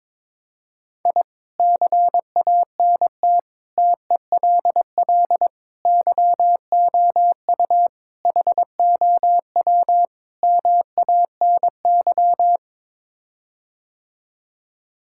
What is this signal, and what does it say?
Morse code, heard as a keyed tone, about 22 wpm, 700 Hz. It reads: I CANT TELL YOU HOW MANY
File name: practice / 22wpm / Sets of 6 Words - 22wpm